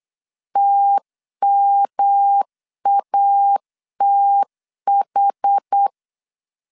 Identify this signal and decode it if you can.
Morse code, heard as a keyed tone, about 8 wpm, 800 Hz. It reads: TMATH
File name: CW / DET / 1 3 1500C S B 8wpm 4141